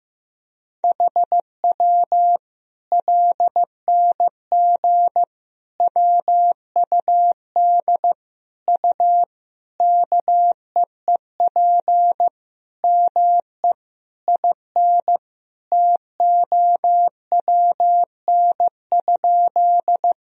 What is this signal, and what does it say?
Morse code, heard as a keyed tone, about 15 wpm, 700 Hz. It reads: HW LNG WUD U KEEP ME IN TOWN?
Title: Sentences from Top 500 Words and Abbreviations 15wpm